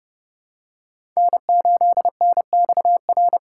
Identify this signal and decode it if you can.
Morse code, heard as a keyed tone, about 30 wpm, 700 Hz. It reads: N8NXR
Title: USA Call Signs - 5 Characters - 3x Character Spacing - 30wpm